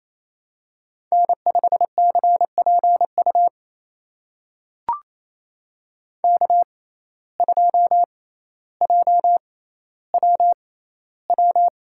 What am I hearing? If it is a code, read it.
Morse code, heard as a keyed tone, about 28 wpm, spaced as for 14 wpm, 700 Hz. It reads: N5CPU K2JWW
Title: USA Call Signs - 5 Characters - 6x Character Spacing - 28wpm